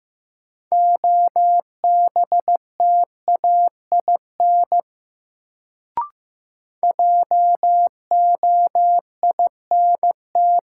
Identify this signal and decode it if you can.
Morse code, heard as a keyed tone, about 15 wpm, 700 Hz. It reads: OBTAIN JOINT
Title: Sentences from Top 1300 Words - New Word Drill - 15wpm